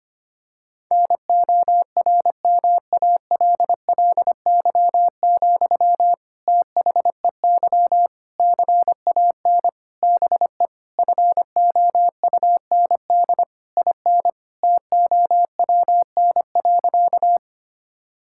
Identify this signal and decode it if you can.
Morse code, heard as a keyed tone, about 25 wpm, 700 Hz. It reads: NORMALLY, THEY CAN BE FOUND IN TOWN.
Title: Sentences from Top 1000 Words - Review Part 6 - 25wpm